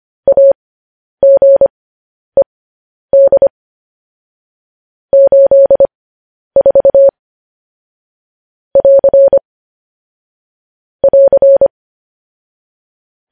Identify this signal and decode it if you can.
Morse code, heard as a keyed tone, about 25 wpm, spaced as for 10 wpm, 550 Hz. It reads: AGED 84 <AR> <AR>
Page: news-headlines-in-morse-code-at-10-wpm